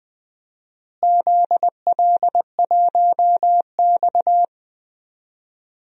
Morse code, heard as a keyed tone, about 19 wpm, 700 Hz. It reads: ZL1X
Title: DXCC-100 Call Signs - 4 Characters - Repeated 3x - 20wpm